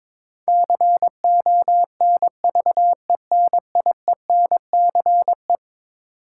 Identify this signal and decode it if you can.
Morse code, heard as a keyed tone, about 22 wpm, 700 Hz. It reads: CONVENIENCE
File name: practice / 22wpm / Eleven Letter Words - 22wpm